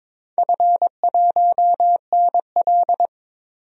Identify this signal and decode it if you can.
Morse code, heard as a keyed tone, about 23 wpm, 700 Hz. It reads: F1NL
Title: DXCC-100 Call Signs - 4 Characters - 22wpm